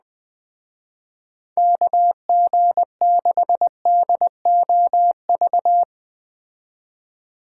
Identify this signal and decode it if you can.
Morse code, heard as a keyed tone, about 21 wpm, 700 Hz. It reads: KG6DOV